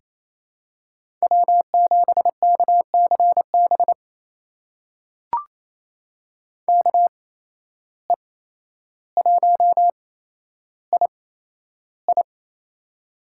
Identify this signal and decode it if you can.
Morse code, heard as a keyed tone, about 28 wpm, spaced as for 12 wpm, 700 Hz. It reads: W7KCB KE1II
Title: USA Call Signs - 5 Characters - 8x Character Spacing - 28wpm